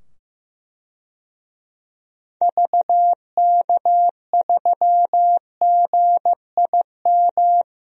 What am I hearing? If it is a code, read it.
Morse code, heard as a keyed tone, about 15 wpm, 700 Hz. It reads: VK3GIM
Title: DXCC-100 Call Signs - 6 Characters - 15wpm